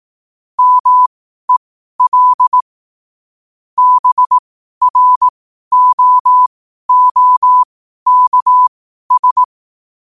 Morse code, heard as a keyed tone, about 18 wpm, spaced as for 12 wpm, 1000 Hz. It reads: MEL BROOKS
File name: qotdfiles / 12wpm